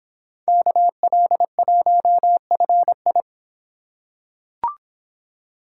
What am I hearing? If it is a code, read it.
Morse code, heard as a keyed tone, about 26 wpm, 700 Hz. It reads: KL1FI